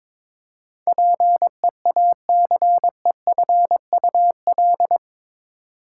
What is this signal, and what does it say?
Morse code, heard as a keyed tone, about 22 wpm, 700 Hz. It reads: PEACEFUL
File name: practice / 22wpm / Eight Letter Words - 22wpm